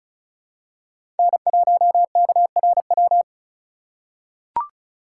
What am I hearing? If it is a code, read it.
Morse code, heard as a keyed tone, about 36 wpm, 700 Hz. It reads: N1KRW
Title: USA Call Signs - 3x Character Spacing - 35wpm